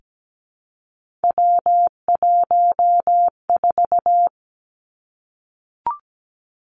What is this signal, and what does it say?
Morse code, heard as a keyed tone, about 17 wpm, 700 Hz. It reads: W14
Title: Three Challenging Letter-Number Combinations 17wpm